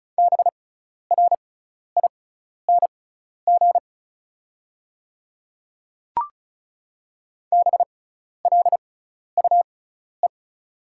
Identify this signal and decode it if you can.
Morse code, heard as a keyed tone, about 34 wpm, spaced as for 12 wpm, 690 Hz. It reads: BRING BLUE
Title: 6x Character Spacing - Top 500 Words - 35wpm